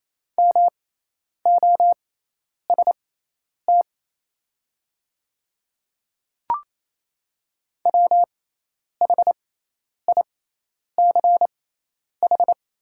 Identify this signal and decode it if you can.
Morse code, heard as a keyed tone, about 28 wpm, spaced as for 10 wpm, 700 Hz. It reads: MOST WHICH
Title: 6x Character Spacing - Top 100 Words - 28wpm